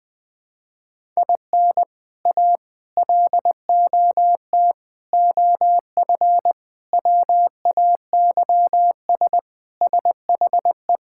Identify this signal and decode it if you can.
Morse code, heard as a keyed tone, about 20 wpm, 700 Hz. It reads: IN A LOT OF WAYS SHE
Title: Sentences with 4-Word Phrases - 20wpm